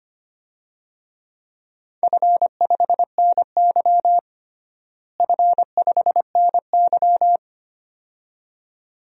Morse code, heard as a keyed tone, about 25 wpm, 700 Hz. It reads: F5NY F5NY
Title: DXCC-100 Call Signs - 4 Characters - Repeated 2x - 25wpm